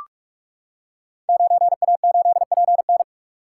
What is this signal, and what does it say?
Morse code, heard as a keyed tone, about 44 wpm, 700 Hz. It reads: 9A8PN
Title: Code Groups - 5 Letter:Numbers - 45wpm